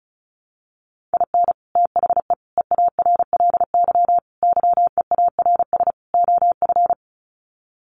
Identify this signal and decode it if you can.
Morse code, heard as a keyed tone, about 34 wpm, 700 Hz. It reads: IN THE EARLY YEARS OF
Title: Sets of 5 Words Encore 35wpm